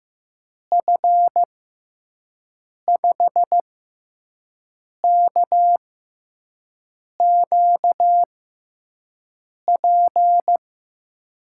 Morse code, heard as a keyed tone, about 15 wpm, spaced as for 5 wpm, 700 Hz. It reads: F5KQP
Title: USA Call Signs - 6 Characters - 6x Character Spacing - 15wpm